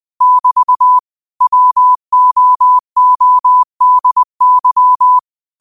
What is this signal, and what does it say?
Morse code, heard as a keyed tone, about 20 wpm, 1000 Hz. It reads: <BT> WOODY
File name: qotdfiles / 20wpm